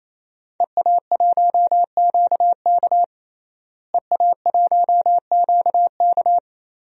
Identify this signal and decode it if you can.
Morse code, heard as a keyed tone, about 28 wpm, 700 Hz. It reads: EA1QK EA1QK
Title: DXCC-100 Call Signs - 5 Characters - Repeated 3x - 28wpm